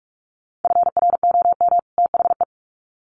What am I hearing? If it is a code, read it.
Morse code, heard as a keyed tone, about 44 wpm, 700 Hz. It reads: FROM THE
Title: Sets of 2 Words 45wpm